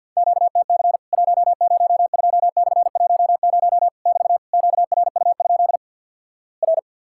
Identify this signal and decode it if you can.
Morse code, heard as a keyed tone, about 50 wpm, 700 Hz. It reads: QTC 102/10 <BT> QRU? R
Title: Worked All Europe DX Contest - QTC Exchange - 50wpm